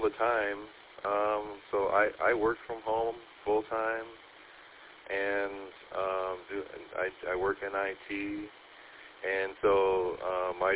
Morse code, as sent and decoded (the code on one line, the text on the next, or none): none